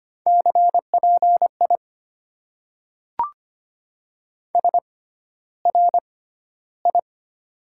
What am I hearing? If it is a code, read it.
Morse code, heard as a keyed tone, about 25 wpm, spaced as for 13 wpm, 690 Hz. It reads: CPI SRI
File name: Top 90 QSO Elements - 6x Character Spacing - 25wpm